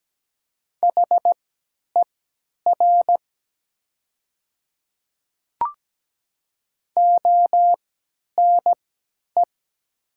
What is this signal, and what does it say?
Morse code, heard as a keyed tone, about 17 wpm, spaced as for 10 wpm, 700 Hz. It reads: HER ONE
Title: 3x Character Spacing - Top 100 Words - 17wpm